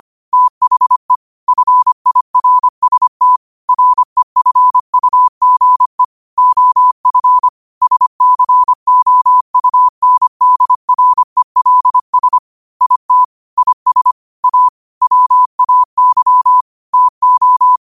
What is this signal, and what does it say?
Morse code, heard as a keyed tone, about 25 wpm, 1000 Hz. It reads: THE FIRST REFUGE OF SCOUNDRELS IT IS A WAY TO
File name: qotdfiles / 25wpm